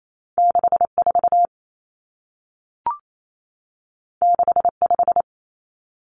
Morse code, heard as a keyed tone, about 28 wpm, 700 Hz. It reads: 64 65
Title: Common Cut Numbers - 2-Digit 28wpm